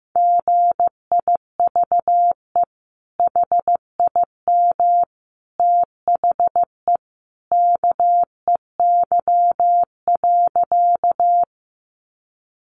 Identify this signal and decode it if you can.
Morse code, heard as a keyed tone, about 15 wpm, 700 Hz. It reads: GIVE HIM THE KEY.
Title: Sentences from Top 600 Words 15wpm